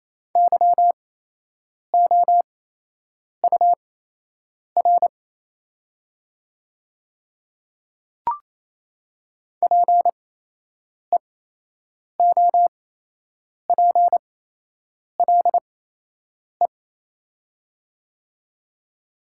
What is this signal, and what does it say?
Morse code, heard as a keyed tone, about 28 wpm, spaced as for 8 wpm, 700 Hz. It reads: YOUR PEOPLE
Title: 8x Character Spacing - Top 100 Words - 28wpm